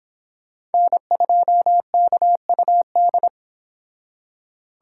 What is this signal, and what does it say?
Morse code, heard as a keyed tone, about 31 wpm, 700 Hz. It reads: N2KUD